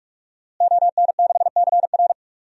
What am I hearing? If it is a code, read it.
Morse code, heard as a keyed tone, about 46 wpm, 700 Hz. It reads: ON6CR
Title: DXCC-100 Call Signs - 5 Characters - Repeated 2x - 45wpm